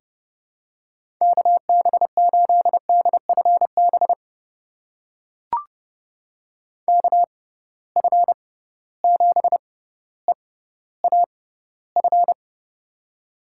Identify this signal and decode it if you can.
Morse code, heard as a keyed tone, about 30 wpm, spaced as for 15 wpm, 700 Hz. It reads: KB8DFB KF7EAF
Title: USA Call Signs - 6 Characters - 6x Character Spacing - 30wpm